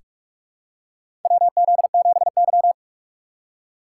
Morse code, heard as a keyed tone, about 44 wpm, 700 Hz. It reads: WZ7Y
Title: USA Call Signs - 4 Characters - 3x Character Spacing - 45wpm